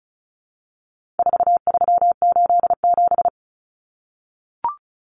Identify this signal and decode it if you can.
Morse code, heard as a keyed tone, about 34 wpm, 700 Hz. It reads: 4387
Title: Numbers - 4-Digit 35wpm